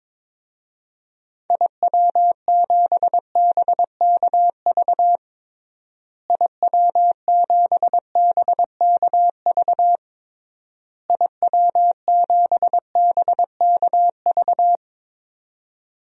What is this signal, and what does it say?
Morse code, heard as a keyed tone, about 22 wpm, 700 Hz. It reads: IW7BKV IW7BKV IW7BKV
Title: DXCC-100 Call Signs - 6 Characters - Repeated 3x - 22wpm